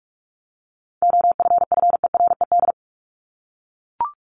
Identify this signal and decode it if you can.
Morse code, heard as a keyed tone, about 44 wpm, 700 Hz. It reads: OFFERED
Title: ED Suffix - Top 100 Words with Base Words 45wpm